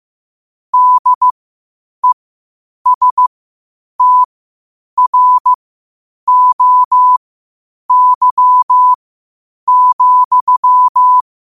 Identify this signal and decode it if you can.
Morse code, heard as a keyed tone, about 15 wpm, spaced as for 8 wpm, 1000 Hz. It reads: DESTROY,
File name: qotdfiles / 8wpm